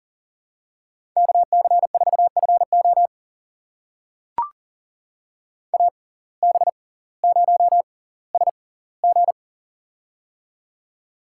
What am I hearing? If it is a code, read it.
Morse code, heard as a keyed tone, about 40 wpm, spaced as for 20 wpm, 700 Hz. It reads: KC4FO AB0SG
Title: USA Call Signs - 5 Characters - 6x Character Spacing - 40wpm